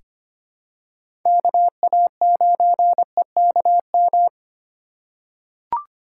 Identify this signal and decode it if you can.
Morse code, heard as a keyed tone, about 25 wpm, 700 Hz. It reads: KA9EKM
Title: USA Call Signs - 6 Characters - 4x Character Spacing - 25wpm